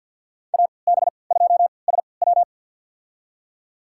Morse code, heard as a keyed tone, about 48 wpm, spaced as for 28 wpm, 690 Hz. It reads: AB2SW
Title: USA Call Signs - 5 Characters - 3x Character Spacing - 50wpm